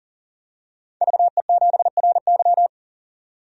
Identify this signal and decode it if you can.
Morse code, heard as a keyed tone, about 40 wpm, 700 Hz. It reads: VE7RY
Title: DXCC-100 Call Signs - 5 Characters - 40wpm